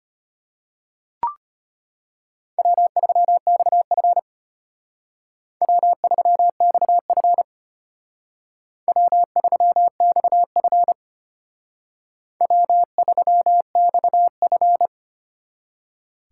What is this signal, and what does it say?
Morse code, heard as a keyed tone, about 32 wpm, 700 Hz. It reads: W3XF W3XF W3XF W3XF